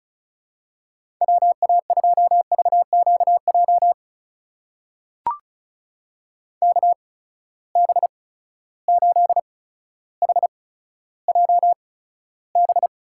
Morse code, heard as a keyed tone, about 36 wpm, spaced as for 15 wpm, 700 Hz. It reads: WA2VQJ KB8HJB
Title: USA Call Signs - 6 Characters - 8x Character Spacing - 35wpm